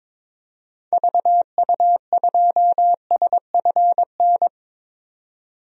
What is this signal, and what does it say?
Morse code, heard as a keyed tone, about 22 wpm, 700 Hz. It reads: VU2SFN
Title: DXCC-100 Call Signs - 6 Characters - 22wpm